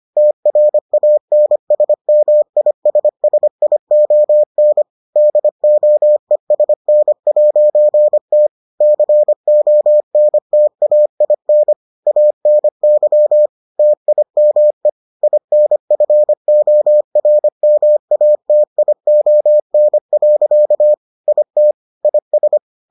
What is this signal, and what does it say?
Morse code, heard as a keyed tone, about 25 wpm, 600 Hz. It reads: TRANSMISSION DOESN'T CONTAIN ANY TIME INFORMATION. IT IS